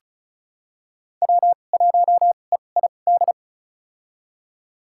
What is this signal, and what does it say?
Morse code, heard as a keyed tone, about 36 wpm, spaced as for 25 wpm, 700 Hz. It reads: W1EID
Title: USA Call Signs - 5 Characters - 2x Character Spacing - 35wpm